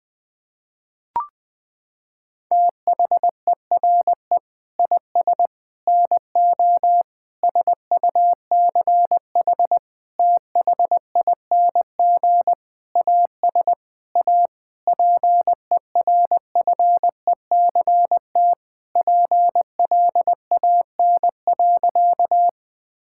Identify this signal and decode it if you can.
Morse code, heard as a keyed tone, about 20 wpm, 700 Hz. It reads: THERE IS NO SUCH THING AS A PERFECT PLAN.
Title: Sentences with 7-Word Phrases - 20wpm